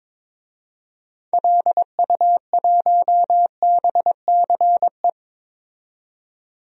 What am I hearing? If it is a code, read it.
Morse code, heard as a keyed tone, about 22 wpm, 700 Hz. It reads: LU1BCE
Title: DXCC-100 Call Signs - 6 Characters - 22wpm